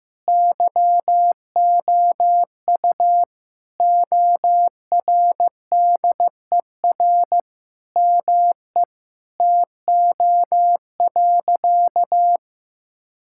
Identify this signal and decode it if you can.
Morse code, heard as a keyed tone, about 16 wpm, 700 Hz. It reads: YOU ORDER ME TO.
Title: Sentences from Top 1000 Words - Review Part 6 - 15wpm